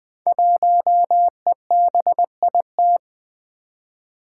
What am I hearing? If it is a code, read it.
Morse code, heard as a keyed tone, about 20 wpm, 700 Hz. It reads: 1EBIT
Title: Five Challenging Letter-Number Combinations 20wpm